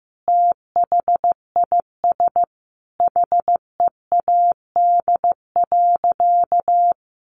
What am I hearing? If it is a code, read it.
Morse code, heard as a keyed tone, about 15 wpm, 700 Hz. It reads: THIS HEAD.
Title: Sentences from Top 1500 Words 15wpm